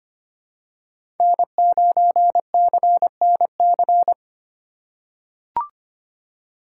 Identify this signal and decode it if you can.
Morse code, heard as a keyed tone, about 25 wpm, 700 Hz. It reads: N9CNC
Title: USA Call Signs - 5 Characters - 2x Character Spacing - 25wpm